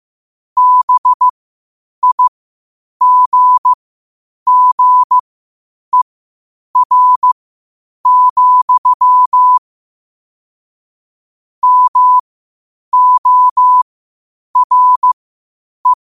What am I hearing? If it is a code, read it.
Morse code, heard as a keyed tone, about 15 wpm, spaced as for 8 wpm, 1000 Hz. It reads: BIGGER, MORE
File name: qotdfiles / 8wpm